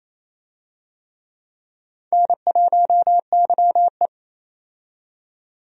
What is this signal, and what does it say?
Morse code, heard as a keyed tone, about 28 wpm, 700 Hz. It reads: N1YE
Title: USA Call Signs - 4 Characters - 28wpm